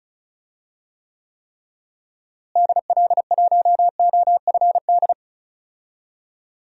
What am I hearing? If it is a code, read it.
Morse code, heard as a keyed tone, about 34 wpm, 700 Hz. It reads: DL1OFD
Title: DXCC-100 Call Signs - 6 Characters - Repeated 3x - 35wpm